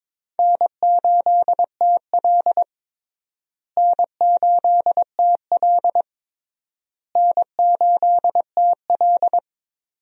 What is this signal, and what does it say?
Morse code, heard as a keyed tone, about 22 wpm, 700 Hz. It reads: N8TL N8TL N8TL
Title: USA Call Signs - 4 Characters - Repeated 3x - 22wpm